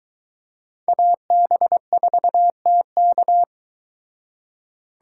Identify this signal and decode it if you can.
Morse code, heard as a keyed tone, about 22 wpm, 700 Hz. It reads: AB4TK